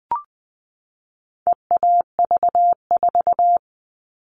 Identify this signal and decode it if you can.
Morse code, heard as a keyed tone, about 20 wpm, 700 Hz. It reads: EAV4
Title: All Cut Numbers - 4-Digit 20wpm